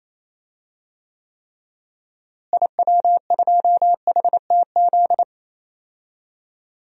Morse code, heard as a keyed tone, about 28 wpm, 700 Hz. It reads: IW2HTZ
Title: DXCC-100 Call Signs - 6 Characters - Repeated 3x - 28wpm